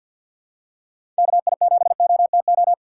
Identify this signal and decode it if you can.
Morse code, heard as a keyed tone, about 46 wpm, 700 Hz. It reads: KI7OTO
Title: USA Call Signs - 6 Characters - 3x Character Spacing - 50wpm